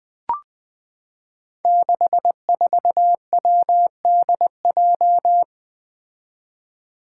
Morse code, heard as a keyed tone, about 20 wpm, 700 Hz. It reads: 64WDJ
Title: Five Challenging Letter-Number Combinations 20wpm